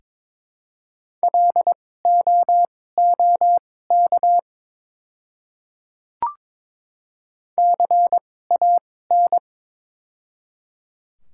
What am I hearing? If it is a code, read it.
Morse code, heard as a keyed tone, about 22 wpm, spaced as for 16 wpm, 700 Hz. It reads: LOOK CAN
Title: 2x Character Spacing - Top 100 Words - 22wpm